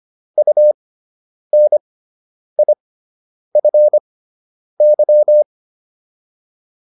Morse code, heard as a keyed tone, about 25 wpm, spaced as for 9 wpm, 600 Hz. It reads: UNIFY